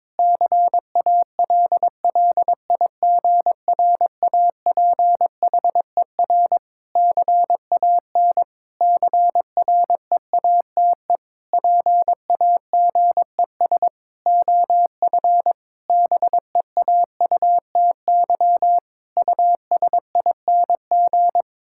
Morse code, heard as a keyed tone, about 21 wpm, 700 Hz. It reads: CALLIGRAPHER CAN CREATE PAGES OF BEAUTY USING
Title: Pangram Challenge v2 - 22wpm